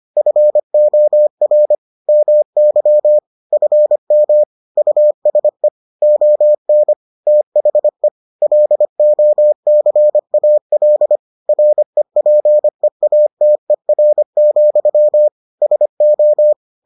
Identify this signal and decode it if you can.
Morse code, heard as a keyed tone, about 25 wpm, 600 Hz. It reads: FOR MY FM USE ON THE LOCAL REPEATER, SO